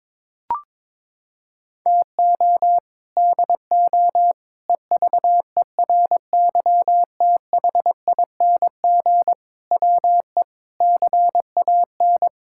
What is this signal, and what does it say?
Morse code, heard as a keyed tone, about 22 wpm, 700 Hz. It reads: TO DO EVERYTHING WE CAN
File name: practice / 22wpm / Sets of 6 Words - 22wpm